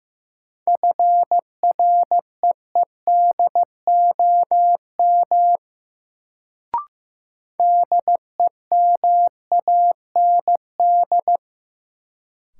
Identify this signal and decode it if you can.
Morse code, heard as a keyed tone, about 15 wpm, 700 Hz. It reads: FREEDOM DEMAND
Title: Sentences from Top 800 Words - New Word Drill - 15wpm